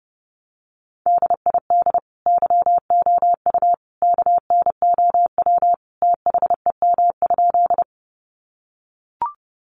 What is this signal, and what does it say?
Morse code, heard as a keyed tone, about 30 wpm, 700 Hz. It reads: DID YOU KNOW THEM?
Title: Sentences from Top 100 Words 30wpm